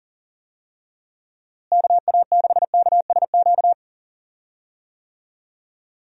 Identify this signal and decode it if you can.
Morse code, heard as a keyed tone, about 40 wpm, 700 Hz. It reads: KA6KSQ
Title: USA Call Signs - 6 Characters - Repeated 3x - 40wpm